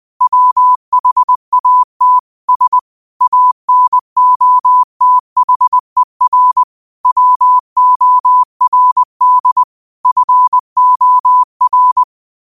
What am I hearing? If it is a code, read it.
Morse code, heard as a keyed tone, about 20 wpm, 1000 Hz. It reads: WHAT S ANOTHER WORD FOR
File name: qotdfiles / 20wpm